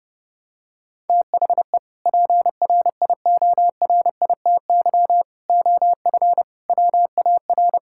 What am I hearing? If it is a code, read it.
Morse code, heard as a keyed tone, about 30 wpm, 700 Hz. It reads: THE PRIORITY OF WAR
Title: Sentences from Top 1600 Words 30wpm